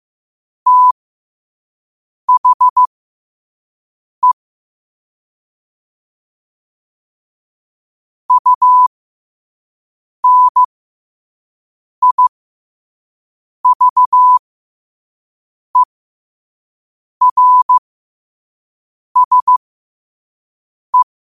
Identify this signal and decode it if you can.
Morse code, heard as a keyed tone, about 15 wpm, spaced as for 5 wpm, 1000 Hz. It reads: THE UNIVERSE
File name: qotdfiles / 5wpm